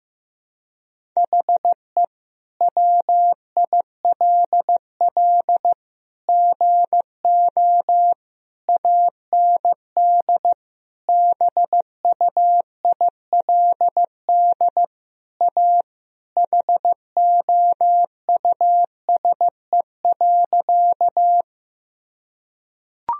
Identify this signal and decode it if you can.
Morse code, heard as a keyed tone, about 15 wpm, 700 Hz. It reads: HE WILL GO AND BUILD A HOUSE.
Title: Sentences from Top 1000 Words - Review Part 1 - 15wpm